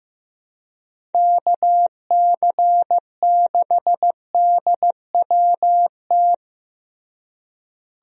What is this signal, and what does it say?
Morse code, heard as a keyed tone, about 15 wpm, 700 Hz. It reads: KC6DWT
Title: USA Call Signs - 6 Characters - Repeated 2x - 15wpm